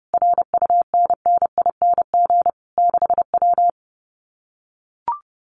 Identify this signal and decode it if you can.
Morse code, heard as a keyed tone, about 29 wpm, 700 Hz. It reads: RUNNING 6W
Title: Common QSO Phrases Encore 30wpm